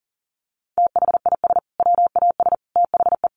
Morse code, heard as a keyed tone, about 40 wpm, 700 Hz. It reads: THIS WAS THE
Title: Sentences from Top 1500 Words 40wpm